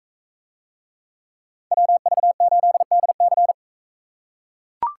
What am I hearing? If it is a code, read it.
Morse code, heard as a keyed tone, about 38 wpm, 700 Hz. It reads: WV8DC